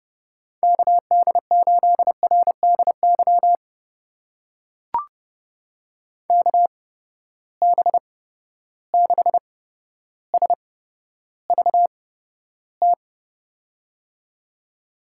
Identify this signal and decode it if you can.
Morse code, heard as a keyed tone, about 29 wpm, spaced as for 13 wpm, 700 Hz. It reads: KD8RDY KB6SVT
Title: USA Call Signs - 6 Characters - 8x Character Spacing - 30wpm